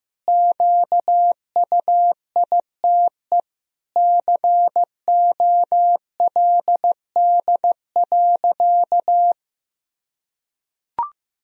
Morse code, heard as a keyed tone, about 15 wpm, 700 Hz. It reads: QUITE COLD.
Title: Sentences from Top 1000 Words - Review Part 4 - 15wpm